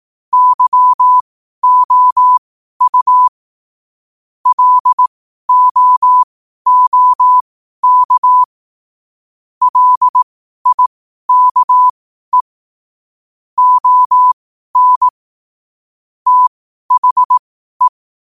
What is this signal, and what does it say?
Morse code, heard as a keyed tone, about 18 wpm, spaced as for 12 wpm, 1000 Hz. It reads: YOU LOOK LIKE ON THE
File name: qotdfiles / 12wpm